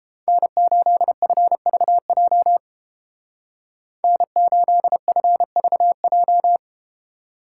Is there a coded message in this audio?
Yes: Morse code from 0.3 to 6.6 s, heard as a keyed tone, about 32 wpm, 700 Hz. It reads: N8FVJ N8FVJ